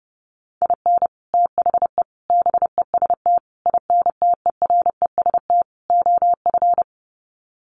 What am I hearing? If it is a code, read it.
Morse code, heard as a keyed tone, about 30 wpm, 700 Hz. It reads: IN THE BEST INTEREST OF
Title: Sets of 5 Words Encore 30wpm